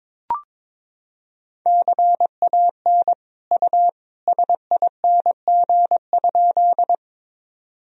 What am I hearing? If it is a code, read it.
Morse code, heard as a keyed tone, about 22 wpm, 700 Hz. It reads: CAN U SING?
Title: Sentences from Top 400 Words and Abbreviations 22wpm